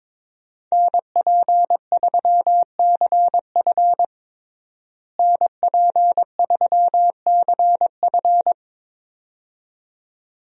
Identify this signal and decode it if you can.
Morse code, heard as a keyed tone, about 21 wpm, 700 Hz. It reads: NP3CF NP3CF
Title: USA Call Signs - 5 Characters - Repeated 2x - 22wpm